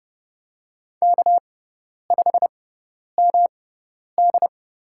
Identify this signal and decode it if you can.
Morse code, heard as a keyed tone, about 30 wpm, spaced as for 10 wpm, 700 Hz. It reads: K5MD
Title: USA Call Signs - 6x Character Spacing - 30wpm